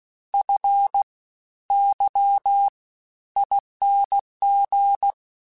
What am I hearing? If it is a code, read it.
Morse code, heard as a keyed tone, about 16 wpm, 800 Hz. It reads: F Y ING
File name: Special INGs-mp3s / Special INGs-mp3s / Fast ing @15